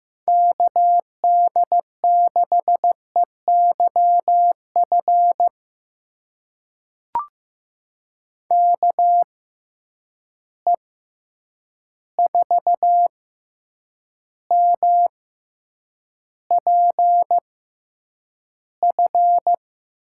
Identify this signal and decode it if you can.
Morse code, heard as a keyed tone, about 15 wpm, spaced as for 8 wpm, 700 Hz. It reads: KD6EYF KE4MPF
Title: USA Call Signs - 6 Characters - 6x Character Spacing - 15wpm